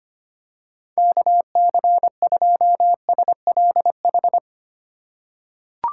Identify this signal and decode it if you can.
Morse code, heard as a keyed tone, about 25 wpm, 700 Hz. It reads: KC2SLH
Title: USA Call Signs - 6 Characters - 4x Character Spacing - 25wpm